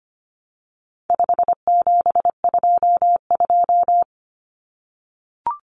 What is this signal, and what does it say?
Morse code, heard as a keyed tone, about 25 wpm, 700 Hz. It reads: 5722
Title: Common Cut Numbers - 4-Digit 25wpm